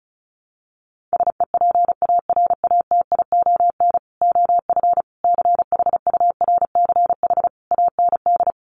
Morse code, heard as a keyed tone, about 34 wpm, 700 Hz. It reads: SEPARATION OF CHURCH AND
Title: Sets of 5 Words 35wpm